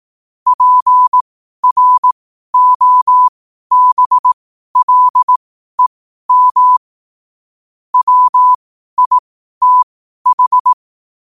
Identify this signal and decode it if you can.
Morse code, heard as a keyed tone, about 18 wpm, spaced as for 12 wpm, 1000 Hz. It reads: PROBLEM WITH
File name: qotdfiles / 12wpm